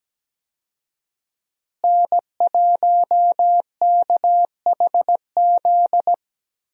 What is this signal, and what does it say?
Morse code, heard as a keyed tone, about 17 wpm, 700 Hz. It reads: N1KHZ
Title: USA Call Signs - 5 Characters - Repeated 2x - 17wpm